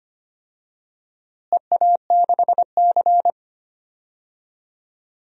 Morse code, heard as a keyed tone, about 25 wpm, 700 Hz. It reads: EA6C
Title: DXCC-100 Call Signs - 4 Characters - Repeated 2x - 25wpm